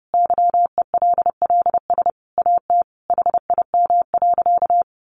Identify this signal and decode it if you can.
Morse code, heard as a keyed tone, about 28 wpm, 700 Hz. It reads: YELLS AT HIM.
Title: Additional Easy Phrases 30wpm